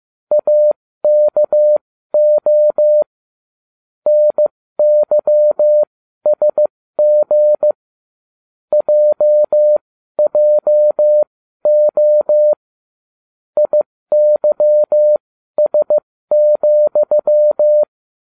Morse code, heard as a keyed tone, about 15 wpm, spaced as for 12 wpm, 600 Hz. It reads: AKO NYSG JJO IYS,